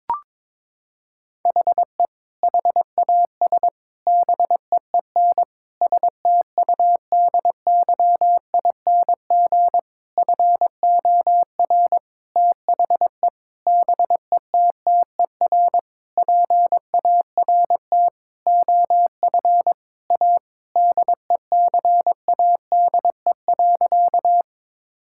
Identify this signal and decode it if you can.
Morse code, heard as a keyed tone, about 21 wpm, 700 Hz. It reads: HE HAS BEEN STUDYING FOR THE BETTER PART OF A DECADE.
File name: Sentences with 7-Word Phrases - 22wpm